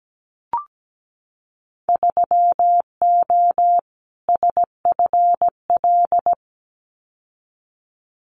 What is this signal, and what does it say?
Morse code, heard as a keyed tone, about 17 wpm, 700 Hz. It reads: 3O SFL
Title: Winter Field Day Exchanges 17wpm